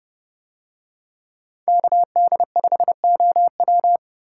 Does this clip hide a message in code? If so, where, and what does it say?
Yes: Morse code from 1.7 to 4.0 s, heard as a keyed tone, about 30 wpm, 700 Hz. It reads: KD5OW